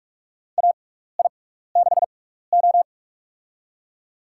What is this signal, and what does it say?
Morse code, heard as a keyed tone, about 43 wpm, spaced as for 15 wpm, 700 Hz. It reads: AI6O
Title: USA Call Signs - 6x Character Spacing - 45wpm